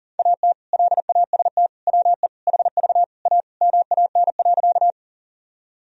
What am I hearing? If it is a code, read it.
Morse code, heard as a keyed tone, about 37 wpm, 700 Hz. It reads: AT LAST WE HV A MAN.
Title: Sentences from Top 300 Words and Abbreviations 40wpm